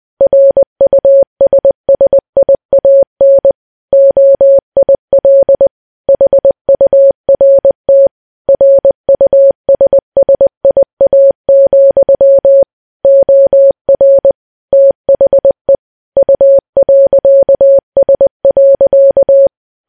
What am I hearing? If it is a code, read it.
Morse code, heard as a keyed tone, about 20 wpm, 550 Hz. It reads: RUSSIAN OIL HURT RUSSIA, OR THE U.S.
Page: news-headlines-in-morse-code-at-20-wpm